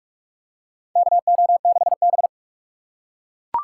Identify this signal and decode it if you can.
Morse code, heard as a keyed tone, about 42 wpm, 700 Hz. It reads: KO6B
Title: USA Call Signs - 4 Characters - 2x Character Spacing - 45wpm